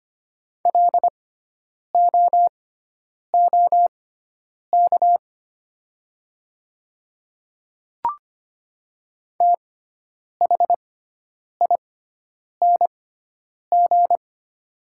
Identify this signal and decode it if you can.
Morse code, heard as a keyed tone, about 25 wpm, spaced as for 9 wpm, 700 Hz. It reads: LOOK THING
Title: 6x Character Spacing - Top 100 Words - 25wpm